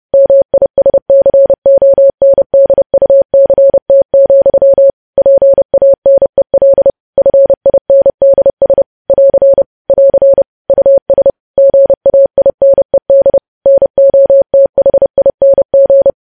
Morse code, heard as a keyed tone, about 30 wpm, 550 Hz. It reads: MISCONDUCT, PANEL FINDS <AR> <AR> US GAINED NOTHING
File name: podcast / News headlines-1x-30x30